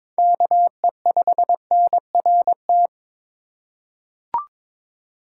Morse code, heard as a keyed tone, about 22 wpm, 700 Hz. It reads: KE5NRT
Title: USA Call Signs - 6 Characters - 4x Character Spacing - 22wpm